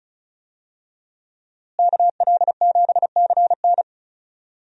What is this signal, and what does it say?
Morse code, heard as a keyed tone, about 36 wpm, 700 Hz. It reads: KL7CN
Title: USA Call Signs - 5 Characters - Repeated 2x - 35wpm